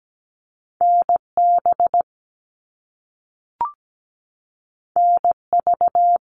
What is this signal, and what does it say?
Morse code, heard as a keyed tone, about 17 wpm, 700 Hz. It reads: NB NV
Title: All Cut Numbers - 2-Digit 17wpm